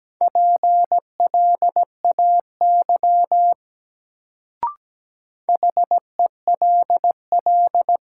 Morse code, heard as a keyed tone, about 17 wpm, 700 Hz. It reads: PLAY HELL
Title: Sets of 5 Words Encore - New Word Drill - 17wpm